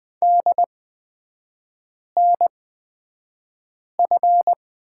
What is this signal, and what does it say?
Morse code, heard as a keyed tone, about 20 wpm, spaced as for 5 wpm, 700 Hz. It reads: DNF
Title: 250 Lesson 40e - BK - Three Characters 20wpm